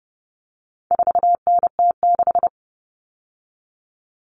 Morse code, heard as a keyed tone, about 30 wpm, 700 Hz. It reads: 4NT6
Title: All Cut Numbers - 4-Digit 30wpm